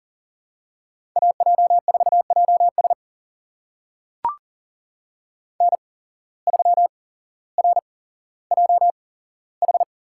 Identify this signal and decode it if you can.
Morse code, heard as a keyed tone, about 40 wpm, spaced as for 17 wpm, 700 Hz. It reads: AJ4JS N3RJH
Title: USA Call Signs - 5 Characters - 8x Character Spacing - 40wpm